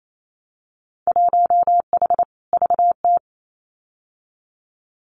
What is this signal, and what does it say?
Morse code, heard as a keyed tone, about 28 wpm, 700 Hz. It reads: 1H VT
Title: Winter Field Day Exchanges 28wpm